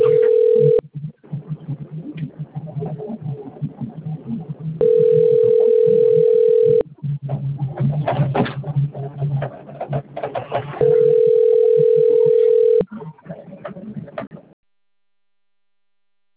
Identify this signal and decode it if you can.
Ringback tone (North America), 440 + 480 Hz.